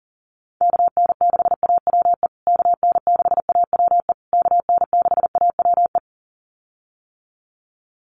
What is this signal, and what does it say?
Morse code, heard as a keyed tone, about 40 wpm, 700 Hz. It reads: KN6AWE KN6AWE KN6AWE
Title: USA Call Signs Repeated 3x 40wpm